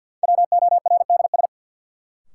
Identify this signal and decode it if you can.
Morse code, heard as a keyed tone, about 53 wpm, 700 Hz. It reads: WORDS
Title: Sets of 3 Words - New Word Drill - 50wpm